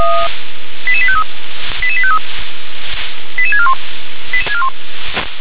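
Five-tone selective call (ZVEI), heard as quick runs of five tones, tones of about 70 ms. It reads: 88853 88853 88641
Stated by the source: SDR reception of a ZVEI five-tone selective call.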